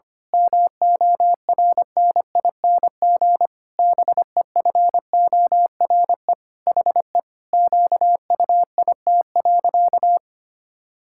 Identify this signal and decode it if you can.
Morse code, heard as a keyed tone, about 25 wpm, 700 Hz. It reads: MORNING BEFORE HE QUIT.
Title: Pangram Challenge v2 - 25wpm